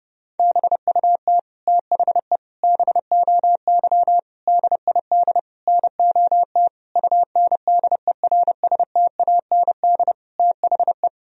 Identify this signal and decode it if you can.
Morse code, heard as a keyed tone, about 30 wpm, 700 Hz. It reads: BUT THE BOY DID NOT UNDERSTAND THE
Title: Sentences from Top 1000 Words - Review Part 3 - 30wpm